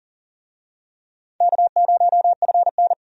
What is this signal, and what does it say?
Morse code, heard as a keyed tone, about 40 wpm, 700 Hz. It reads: K0FN
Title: USA Call Signs - 4 Characters - 40wpm